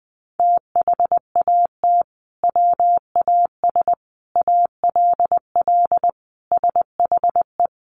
Morse code, heard as a keyed tone, about 20 wpm, 700 Hz. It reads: THAT WAS ALL SHE
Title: Sentences from Top 100 Words 20wpm